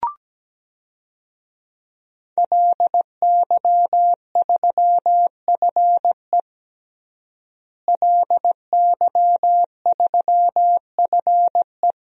Morse code, heard as a keyed tone, about 17 wpm, 700 Hz. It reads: LY3FE LY3FE
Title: DXCC-100 Call Signs - 5 Characters - Repeated 3x - 17wpm